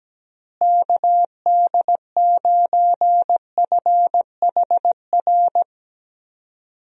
Morse code, heard as a keyed tone, about 17 wpm, 700 Hz. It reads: KD9FHR
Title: USA Call Signs - 6 Characters - Repeated 2x - 17wpm